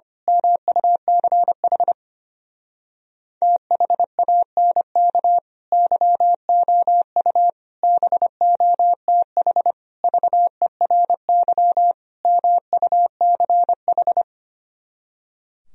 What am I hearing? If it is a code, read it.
Morse code, heard as a keyed tone, about 25 wpm, 700 Hz. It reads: MUCH THANK YOU BOTH VERY MUCH
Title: Speed-Racing - Sets of 5 Words 25wpm